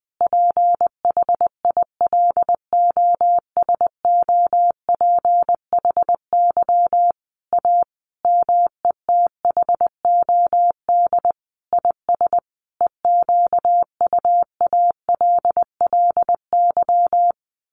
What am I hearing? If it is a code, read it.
Morse code, heard as a keyed tone, about 20 wpm, 700 Hz. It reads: PHILOSOPHY A METHOD IS EQUALLY
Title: Sentences from Top 1400 Words 20wpm